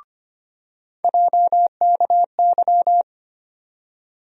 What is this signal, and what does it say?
Morse code, heard as a keyed tone, about 25 wpm, 700 Hz. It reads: JKY
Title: USA Call Sign Postfixes - 3 Letters 25wpm